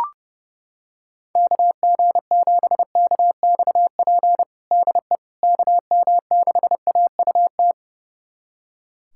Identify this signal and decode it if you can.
Morse code, heard as a keyed tone, about 30 wpm, 700 Hz. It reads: KG7KXP DE KM6AUT
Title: USA Callsign DE Callsign - 6 Characters - 30wpm